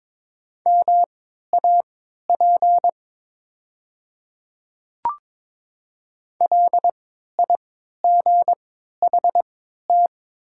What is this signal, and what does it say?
Morse code, heard as a keyed tone, about 22 wpm, spaced as for 12 wpm, 700 Hz. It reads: MAP LIGHT
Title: 3x Character Spacing - Top 500 Words - 22wpm